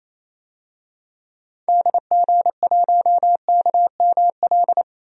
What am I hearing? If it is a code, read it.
Morse code, heard as a keyed tone, about 28 wpm, 700 Hz. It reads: DG1KML
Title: DXCC-100 Call Signs - 6 Characters - 28wpm